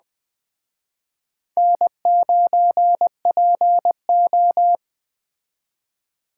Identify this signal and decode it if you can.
Morse code, heard as a keyed tone, about 20 wpm, 700 Hz. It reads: N9PO
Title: Speed Racing - USA 1x2 Call Signs 17wpm